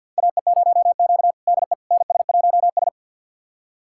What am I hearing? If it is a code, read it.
Morse code, heard as a keyed tone, about 52 wpm, 700 Hz. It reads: AE0Q DE NS1S
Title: USA Callsign DE Callsign - 4 Characters - 50wpm